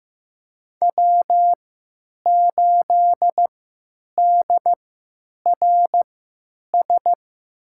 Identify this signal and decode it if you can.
Morse code, heard as a keyed tone, about 15 wpm, spaced as for 9 wpm, 700 Hz. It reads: W8DRS
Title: USA Call Signs - 5 Characters - 3x Character Spacing - 15wpm